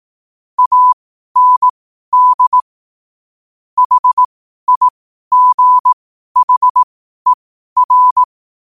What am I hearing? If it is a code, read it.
Morse code, heard as a keyed tone, about 18 wpm, spaced as for 12 wpm, 1000 Hz. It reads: AND HIGHER